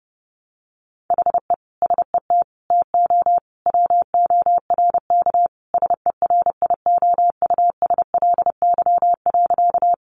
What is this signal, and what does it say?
Morse code, heard as a keyed tone, about 30 wpm, 700 Hz. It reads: HE SET TO WORK SERIOUSLY.
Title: Sentences from Top 1200 Words 30wpm